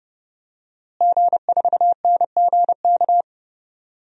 Morse code, heard as a keyed tone, about 32 wpm, 700 Hz. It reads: G4NGK
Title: DXCC-100 Call Signs - 5 Characters - Repeated 3x - 30wpm